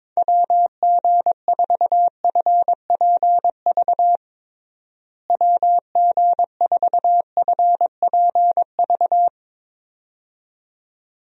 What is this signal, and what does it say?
Morse code, heard as a keyed tone, about 22 wpm, 700 Hz. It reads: WG4FPV WG4FPV
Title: USA Call Signs - 6 Characters - Repeated 2x - 22wpm